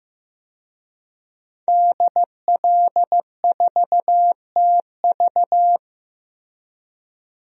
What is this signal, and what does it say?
Morse code, heard as a keyed tone, about 15 wpm, 700 Hz. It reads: DL4TV
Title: DXCC-100 Call Signs - 5 Characters - Repeated 3x - 15wpm